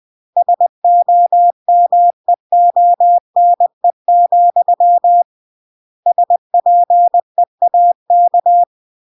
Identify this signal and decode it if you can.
Morse code, heard as a keyed tone, about 20 wpm, 700 Hz. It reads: SOMEONE, SPEAK
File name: CW / jodorowsky / 82 commandments 20-20